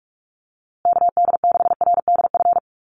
Morse code, heard as a keyed tone, about 44 wpm, 700 Hz. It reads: KD6RDF
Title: USA Call Signs Repeated 3x 45wpm